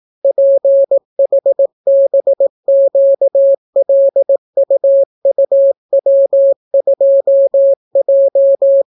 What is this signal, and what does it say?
Morse code, heard as a keyed tone, about 18 wpm, 550 Hz. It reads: PHBQLUUW2J